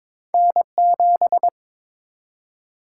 Morse code, heard as a keyed tone, about 23 wpm, 700 Hz. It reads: N7
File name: practice / 22wpm / Two Letter-Number Combinations - 22wpm